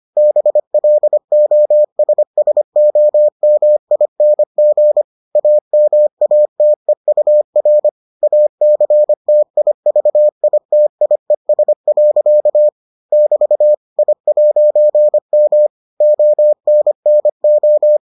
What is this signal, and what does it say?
Morse code, heard as a keyed tone, about 25 wpm, 600 Hz. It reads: BLOSSOMING AMATEUR ACTIVITIES. <BT> I'M ONNO